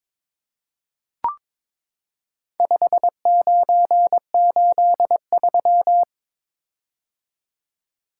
Morse code, heard as a keyed tone, about 22 wpm, 700 Hz. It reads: 5983
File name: practice / 22wpm / Four Letter-Number Combinations - 22wpm